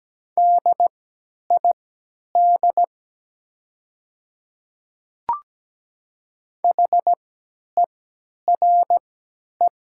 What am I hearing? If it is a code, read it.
Morse code, heard as a keyed tone, about 17 wpm, spaced as for 10 wpm, 700 Hz. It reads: DID HERE
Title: 3x Character Spacing - Top 300 Words - 17wpm